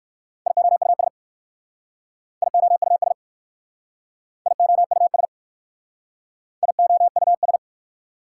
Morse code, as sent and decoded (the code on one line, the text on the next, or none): .. --- ..- ... / .. --- ..- ... / .. --- ..- ... / .. --- ..- ...
IOUS IOUS IOUS IOUS